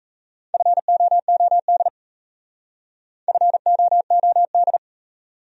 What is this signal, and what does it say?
Morse code, heard as a keyed tone, about 34 wpm, 700 Hz. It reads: FOOD FOOD
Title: Speed-Racing Top 300 Words 28wpm